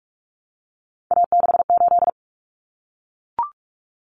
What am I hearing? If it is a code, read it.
Morse code, heard as a keyed tone, about 44 wpm, 700 Hz. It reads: A68
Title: Common Cut Numbers - 3-Digit 45wpm